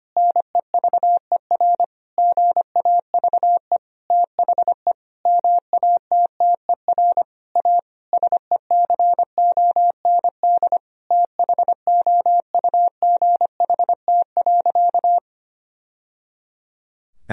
Morse code, heard as a keyed tone, about 25 wpm, 700 Hz. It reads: NEVER GAVE THE MATTER A SECOND THOUGHT.
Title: Sentences from Top 1000 Words - Review Part 2 - 25wpm